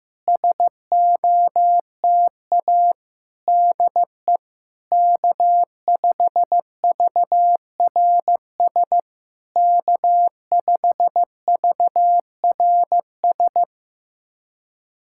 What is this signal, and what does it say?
Morse code, heard as a keyed tone, about 15 wpm, 700 Hz. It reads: SOTA DE K5VRS K5VRS
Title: SOTA QSO Practice 15wpm